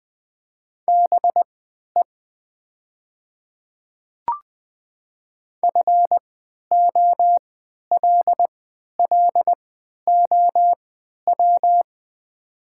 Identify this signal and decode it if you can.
Morse code, heard as a keyed tone, about 20 wpm, spaced as for 11 wpm, 700 Hz. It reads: BE FOLLOW
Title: 3x Character Spacing - Top 200 Words - 20wpm